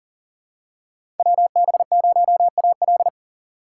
Morse code, heard as a keyed tone, about 40 wpm, 700 Hz. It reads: WB0AL
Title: USA Call Signs - 3x Character Spacing - 40wpm